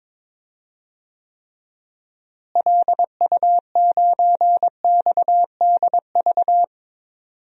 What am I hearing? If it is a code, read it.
Morse code, heard as a keyed tone, about 22 wpm, 700 Hz. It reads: LU9XDV
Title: DXCC-100 Call Signs - 6 Characters - Repeated 3x - 22wpm